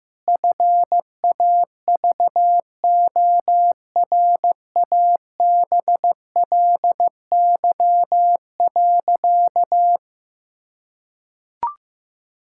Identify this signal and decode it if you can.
Morse code, heard as a keyed tone, about 15 wpm, 700 Hz. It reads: FAVORABLY.
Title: Pangram Challenge v2 - 15wpm